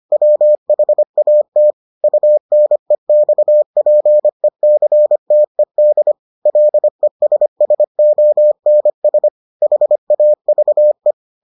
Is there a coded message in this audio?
Yes: Morse code from 0.1 to 11.1 s, heard as a keyed tone, about 25 wpm, 600 Hz. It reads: WHAT UNEXPECTED LESSONS HAVE